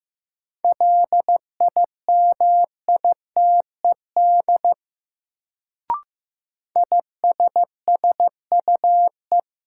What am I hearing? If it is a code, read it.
Morse code, heard as a keyed tone, about 16 wpm, 700 Hz. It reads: LIMITED ISSUE
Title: Sentences from Top 900 Words - New Word Drill - 15wpm